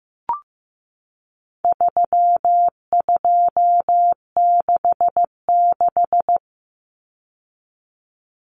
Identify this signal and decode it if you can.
Morse code, heard as a keyed tone, about 15 wpm, 700 Hz. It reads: 3266
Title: Numbers - 4-Digit 15wpm